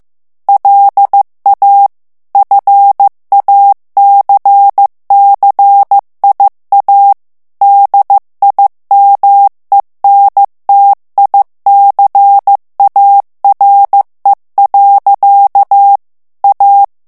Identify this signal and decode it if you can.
Morse code, heard as a keyed tone, about 15 wpm, 800 Hz. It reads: LA FACCIA DIMENTICARE. A
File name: ITALIEN texte clair 1 vitesse 14 mots minute